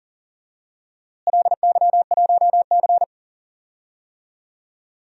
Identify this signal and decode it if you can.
Morse code, heard as a keyed tone, about 37 wpm, 700 Hz. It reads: LY1C
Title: DXCC-100 Call Signs - 4 Characters - Repeated 3x - 40wpm